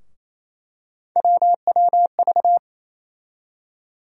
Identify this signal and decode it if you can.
Morse code, heard as a keyed tone, about 28 wpm, 700 Hz. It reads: WWV
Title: USA Call Sign Postfixes - 3 Letters 28wpm